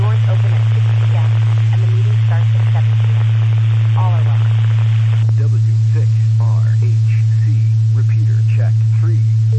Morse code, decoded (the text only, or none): none